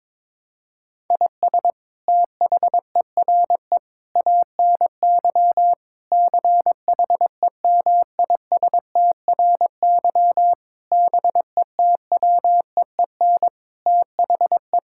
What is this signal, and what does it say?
Morse code, heard as a keyed tone, about 22 wpm, 700 Hz. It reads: IS THERE ANY CHEMISTRY BETWEEN THE
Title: Sentences from Top 1000 Words - Review Part 6 - 22wpm